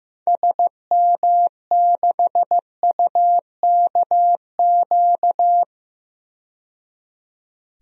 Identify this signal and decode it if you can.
Morse code, heard as a keyed tone, about 15 wpm, 700 Hz. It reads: SM6UKQ
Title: DXCC-100 Call Signs - 6 Characters - 15wpm